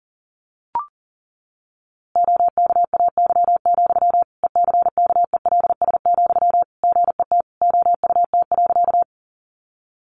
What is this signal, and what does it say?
Morse code, heard as a keyed tone, about 40 wpm, 700 Hz. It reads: OKAY, ECKELS, GET OUT.
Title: A Sound Of Thunder 40wpm